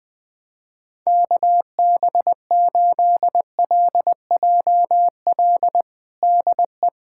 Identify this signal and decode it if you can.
Morse code, heard as a keyed tone, about 21 wpm, 700 Hz. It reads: KB8LJL DE
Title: USA Callsign DE Callsign 20wpm